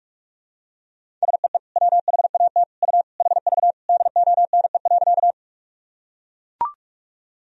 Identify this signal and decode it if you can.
Morse code, heard as a keyed tone, about 42 wpm, 700 Hz. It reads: SEE WHAT U HV DONE.
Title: Sentences from Top 500 Words and Abbreviations 45wpm